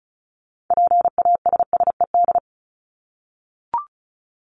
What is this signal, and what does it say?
Morse code, heard as a keyed tone, about 34 wpm, 700 Hz. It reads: PASSED
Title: ED Suffix - Top 100 Words with Base Words 35wpm